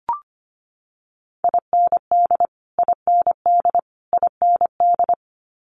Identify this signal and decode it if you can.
Morse code, heard as a keyed tone, about 25 wpm, 700 Hz. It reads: IND IND IND
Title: Multi-letter Phonemes Repeated 3x 25wpm